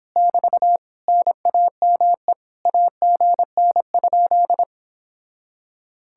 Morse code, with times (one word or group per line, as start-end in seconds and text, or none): 0.2-0.8 <BT>
1.1-2.3 NAME
2.7-4.6 AGN?